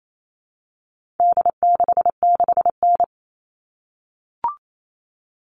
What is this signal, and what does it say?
Morse code, heard as a keyed tone, about 28 wpm, 700 Hz. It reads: D66N
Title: All Cut Numbers - 4-Digit 28wpm